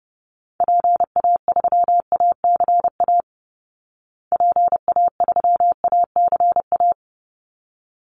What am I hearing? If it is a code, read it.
Morse code, heard as a keyed tone, about 30 wpm, 700 Hz. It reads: PA3ACA PA3ACA
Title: Call Signs Repeated 3x - 30wpm